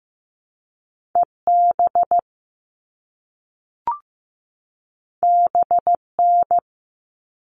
Morse code, heard as a keyed tone, about 15 wpm, 700 Hz. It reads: EB BN
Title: All Cut Numbers - 2-Digit 15wpm